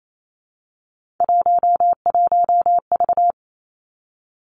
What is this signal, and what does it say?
Morse code, heard as a keyed tone, about 28 wpm, 700 Hz. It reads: 11V
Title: Three Challenging Letter-Number Combinations 28wpm